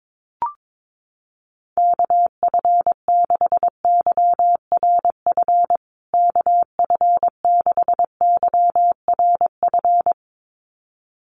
Morse code, heard as a keyed tone, about 22 wpm, 700 Hz. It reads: KF6YRF KF6YRF
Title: USA Call Signs Repeated 2x 22wpm